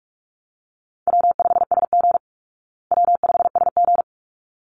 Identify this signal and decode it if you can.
Morse code, heard as a keyed tone, about 44 wpm, 700 Hz. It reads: W5SG W5SG
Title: Easy Call Signs Repeated 2x - 45wpm